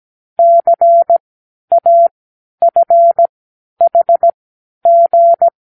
Morse code, heard as a keyed tone, about 17 wpm, spaced as for 10 wpm, 680 Hz. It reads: CAFHG
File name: k7qo / Lesson18(A-L)